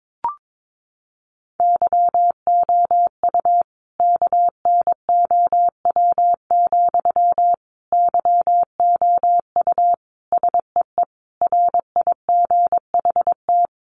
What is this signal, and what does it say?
Morse code, heard as a keyed tone, about 22 wpm, 700 Hz. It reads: YOU KNOW, YOU SEE RIGHT
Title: Sentences from Top 200 Words 22wpm